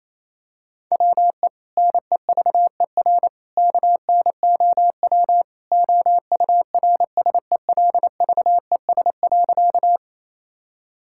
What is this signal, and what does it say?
Morse code, heard as a keyed tone, about 27 wpm, 700 Hz. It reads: WE NEVER KNOW OURSELVES.
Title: Sentences from Top 1600 Words 28wpm